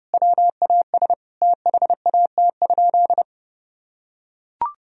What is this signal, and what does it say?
Morse code, heard as a keyed tone, about 30 wpm, 700 Hz. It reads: WAS THAT?
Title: Sentences from Top 100 Words and Abbreviations 30wpm